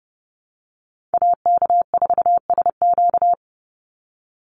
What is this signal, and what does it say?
Morse code, heard as a keyed tone, about 30 wpm, 700 Hz. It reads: AK4SQ
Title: Call Signs - 30wpm